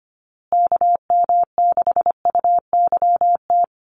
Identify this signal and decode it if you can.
Morse code, heard as a keyed tone, about 25 wpm, 700 Hz. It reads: KM6UYT
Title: USA Call Signs Repeated 2x 25wpm